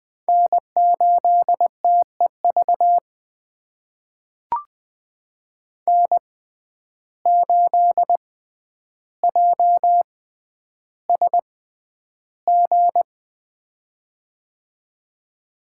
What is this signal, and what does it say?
Morse code, heard as a keyed tone, about 20 wpm, spaced as for 10 wpm, 700 Hz. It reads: N8TEV N8JSG
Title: USA Call Signs - 5 Characters - 6x Character Spacing - 20wpm